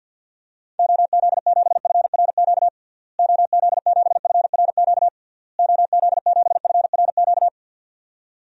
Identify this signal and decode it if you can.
Morse code, heard as a keyed tone, about 51 wpm, 700 Hz. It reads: OZ7FRQ OZ7FRQ OZ7FRQ
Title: DXCC-100 Call Signs - 6 Characters - Repeated 3x - 50wpm